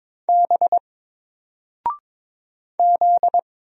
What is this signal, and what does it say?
Morse code, heard as a keyed tone, about 22 wpm, 700 Hz. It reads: B Z
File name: Single-Letter-Number-22wpm